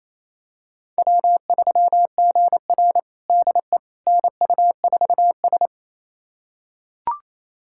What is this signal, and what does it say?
Morse code, heard as a keyed tone, about 28 wpm, 700 Hz. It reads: W3GR DE NU4S
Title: USA Callsign DE Callsign - 4 Characters - 28wpm